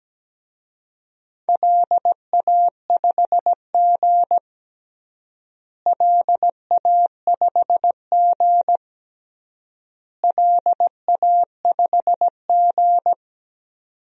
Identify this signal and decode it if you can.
Morse code, heard as a keyed tone, about 17 wpm, 700 Hz. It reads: LA5G LA5G LA5G
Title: DXCC-100 Call Signs - 4 Characters - Repeated 3x - 17wpm